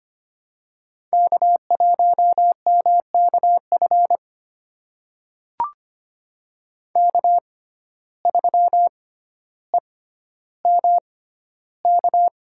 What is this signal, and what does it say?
Morse code, heard as a keyed tone, about 25 wpm, spaced as for 13 wpm, 700 Hz. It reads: K1MKF K3EMK
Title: USA Call Signs - 5 Characters - 6x Character Spacing - 25wpm